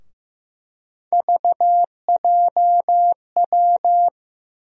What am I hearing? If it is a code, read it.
Morse code, heard as a keyed tone, about 15 wpm, 700 Hz. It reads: VJW
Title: USA Call Sign Postfixes - 3 Letters 15wpm